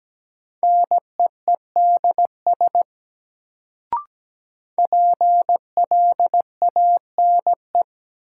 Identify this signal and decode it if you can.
Morse code, heard as a keyed tone, about 17 wpm, 700 Hz. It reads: NEEDS PLANE
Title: Sentences from Top 1000 Words - Review Part 3 - New Word Drill - 17wpm